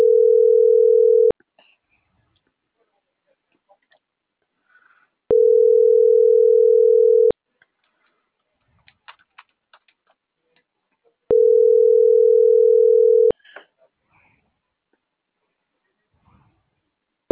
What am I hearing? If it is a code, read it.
Ringback tone (North America), 440 + 480 Hz.